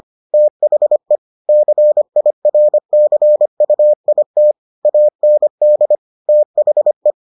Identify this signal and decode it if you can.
Morse code, heard as a keyed tone, about 25 wpm, 600 Hz. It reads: THE CIRCUIT AND THE